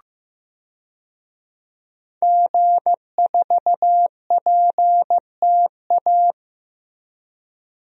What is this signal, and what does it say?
Morse code, heard as a keyed tone, about 16 wpm, 700 Hz. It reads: G4PTA
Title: DXCC-100 Call Signs - 5 Characters - Repeated 3x - 15wpm